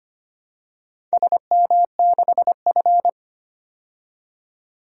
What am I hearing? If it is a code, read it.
Morse code, heard as a keyed tone, about 25 wpm, 700 Hz. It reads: SM6F
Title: DXCC-100 Call Signs - 4 Characters - 25wpm